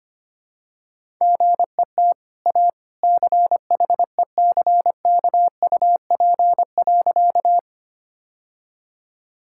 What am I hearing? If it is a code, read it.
Morse code, heard as a keyed tone, about 25 wpm, 700 Hz. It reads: GET A CHECKUP.
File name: Easy Sentences with Hints - 25wpm